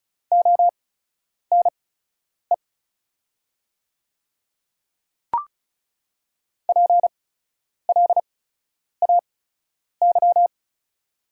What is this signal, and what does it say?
Morse code, heard as a keyed tone, about 36 wpm, spaced as for 10 wpm, 700 Hz. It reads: ONE PLAY
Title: 8x Character Spacing - Top 300 Words - 35wpm